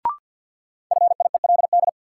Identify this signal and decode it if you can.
Morse code, heard as a keyed tone, about 46 wpm, 700 Hz. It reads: FIELD
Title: Sentences from Top 1000 Words - New Word Drill - 50wpm